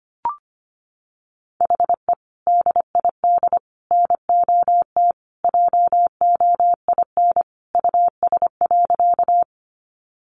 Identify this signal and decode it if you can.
Morse code, heard as a keyed tone, about 25 wpm, 700 Hz. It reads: HE DID NOT JOIN US.
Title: Sentences from Top 1400 Words 25wpm